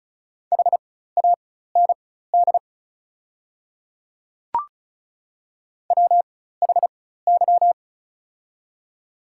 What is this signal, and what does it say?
Morse code, heard as a keyed tone, about 34 wpm, spaced as for 16 wpm, 690 Hz. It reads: HAND WHY
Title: 4x Character Spacing - Top 200 Words - 35wpm